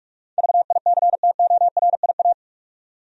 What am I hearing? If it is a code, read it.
Morse code, heard as a keyed tone, about 46 wpm, 700 Hz. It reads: VICTORIA
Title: USA Common Names - 45wpm